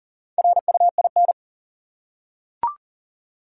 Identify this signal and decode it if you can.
Morse code, heard as a keyed tone, about 40 wpm, 700 Hz. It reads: RUIN
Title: Four Letter Words - 40wpm